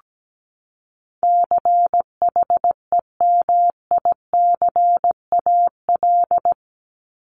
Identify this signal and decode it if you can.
Morse code, heard as a keyed tone, about 17 wpm, 700 Hz. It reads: CHEMICAL
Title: Top 1001-1250 Words 17wpm